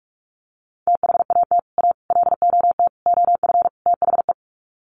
Morse code, heard as a keyed tone, about 44 wpm, 700 Hz. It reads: THAT A LOT OF THE
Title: Sets of 5 Words Encore2 45wpm